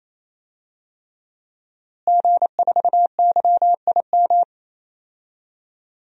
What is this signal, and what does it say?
Morse code, heard as a keyed tone, about 28 wpm, 700 Hz. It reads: G4YIM
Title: DXCC-100 Call Signs - 5 Characters - Repeated 3x - 28wpm